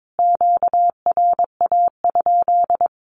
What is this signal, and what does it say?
Morse code, heard as a keyed tone, about 22 wpm, 700 Hz. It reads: QRA?
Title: Q-Codes Repeated 3x 22wpm